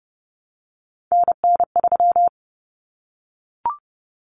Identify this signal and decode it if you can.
Morse code, heard as a keyed tone, about 30 wpm, 700 Hz. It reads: NN3
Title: Common Cut Numbers - 3-Digit 30wpm